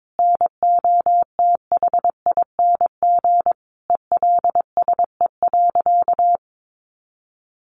Morse code, heard as a keyed tone, about 22 wpm, 700 Hz. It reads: NOTHING ELSE.
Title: Sentences from Top 800 Words 22wpm